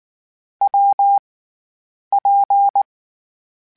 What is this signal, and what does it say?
Morse code, heard as a keyed tone, about 19 wpm, 800 Hz. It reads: W P
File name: Difficult Combos / Difficult Combos-mp3s / WPJ WPJ1 @18